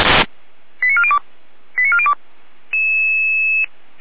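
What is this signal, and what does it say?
Five-tone selective call (ZVEI), heard as quick runs of five tones, tones of about 70 ms. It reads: 89402 89402
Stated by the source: ZVEI five-tone selective call 89403 received via SDR.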